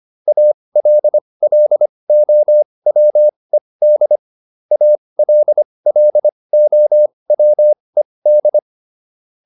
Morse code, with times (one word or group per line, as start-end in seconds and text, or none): 0.3-4.2 ALLOWED
4.7-8.6 ALLOWED